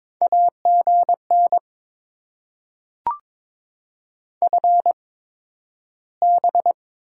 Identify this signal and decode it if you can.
Morse code, heard as a keyed tone, about 22 wpm, spaced as for 12 wpm, 700 Hz. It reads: AGN FB
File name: Top 20 QSO Elements - 8x Character Spacing - 22wpm